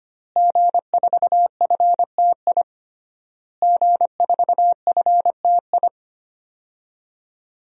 Morse code, heard as a keyed tone, about 25 wpm, 700 Hz. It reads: G4FTI G4FTI
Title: DXCC-100 Call Signs - 5 Characters - Repeated 3x - 25wpm